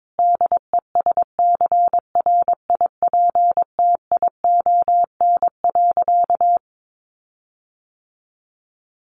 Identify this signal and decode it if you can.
Morse code, heard as a keyed tone, about 22 wpm, 700 Hz. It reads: DESCRIPTION.
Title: Sentences from Top 1300 Words 22wpm